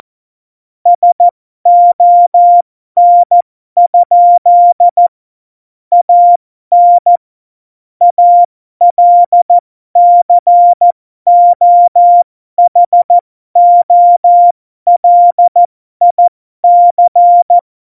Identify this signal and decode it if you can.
Morse code, heard as a keyed tone, about 14 wpm, spaced as for 12 wpm, 700 Hz. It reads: SON? AN ALCOHOLIC
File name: Morsekurs / Witze